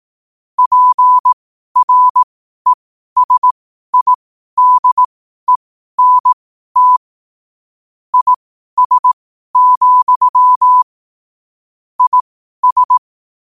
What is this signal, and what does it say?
Morse code, heard as a keyed tone, about 18 wpm, spaced as for 12 wpm, 1000 Hz. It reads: PRESIDENT IS, IS
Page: 12-wpm-morse-code-quotes